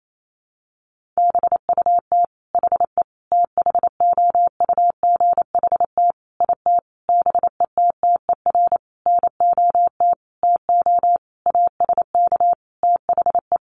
Morse code, heard as a keyed tone, about 28 wpm, 700 Hz. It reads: BUT HE THOUGHT IT BETTER NOT TO ASK THE
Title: Sentences from Top 600 Words 28wpm